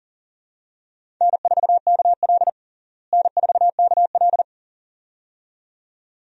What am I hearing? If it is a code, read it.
Morse code, heard as a keyed tone, about 40 wpm, 700 Hz. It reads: N4KL N4KL
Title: USA Call Signs - 4 Characters - Repeated 2x - 40wpm